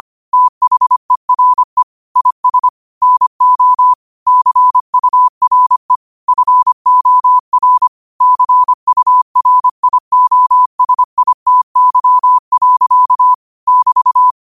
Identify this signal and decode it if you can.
Morse code, heard as a keyed tone, about 25 wpm, 1000 Hz. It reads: THERE IS NO CURE FOR CURIOSITY. <BT>
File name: qotdfiles / 25wpm